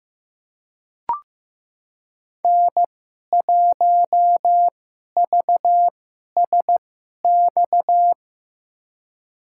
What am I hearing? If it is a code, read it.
Morse code, heard as a keyed tone, about 16 wpm, spaced as for 11 wpm, 700 Hz. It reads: N1VSX
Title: USA Call Signs - 5 Characters - 2x Character Spacing - 15wpm